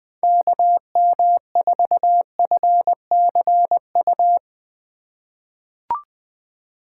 Morse code, heard as a keyed tone, about 20 wpm, 700 Hz. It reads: KM4FCU
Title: USA Call Signs - 6 Characters - 3x Character Spacing - 20wpm